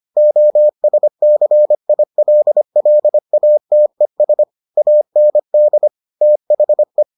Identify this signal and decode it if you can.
Morse code, heard as a keyed tone, about 25 wpm, 600 Hz. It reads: OSCILLATES AND THE